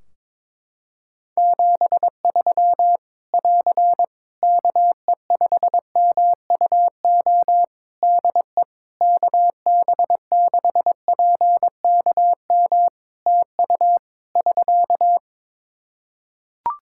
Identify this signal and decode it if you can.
Morse code, heard as a keyed tone, about 22 wpm, 700 Hz. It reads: 73 <AR> KE5MUO DE KB6PKM TU <SK>
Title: Practice QSO - Exchange 4 - Ending - 22wpm